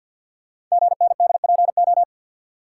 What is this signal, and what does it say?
Morse code, heard as a keyed tone, about 47 wpm, 700 Hz. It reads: GNDPO
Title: Code Groups - 5 Letters - 50wpm